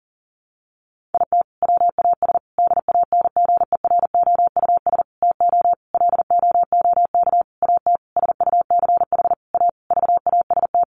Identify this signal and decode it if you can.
Morse code, heard as a keyed tone, about 40 wpm, 700 Hz. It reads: IT WAS DANGEROUS TO LOOK AT SUCH A VAST
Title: Sentences from Top 1300 Words 40wpm